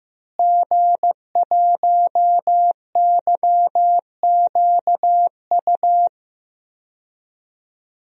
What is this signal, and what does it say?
Morse code, heard as a keyed tone, about 15 wpm, 700 Hz. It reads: G1YQU
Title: DXCC-100 Call Signs - 5 Characters - Repeated 2x - 15wpm